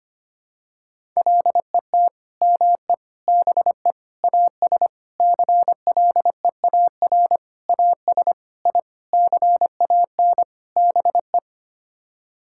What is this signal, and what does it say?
Morse code, heard as a keyed tone, about 25 wpm, 700 Hz. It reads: LET ME BE AS CLEAR AS I CAN BE
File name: Sets of 9 Words - 25wpm